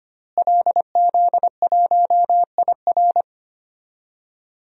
Morse code, heard as a keyed tone, about 25 wpm, 700 Hz. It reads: LZ1IR
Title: DXCC-100 Call Signs - 5 Characters - Repeated 3x - 25wpm